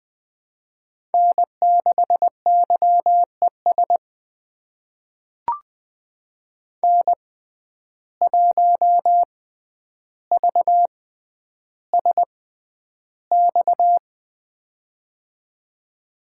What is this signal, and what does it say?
Morse code, heard as a keyed tone, about 20 wpm, spaced as for 10 wpm, 700 Hz. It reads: N6YES N1VSX
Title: USA Call Signs - 5 Characters - 6x Character Spacing - 20wpm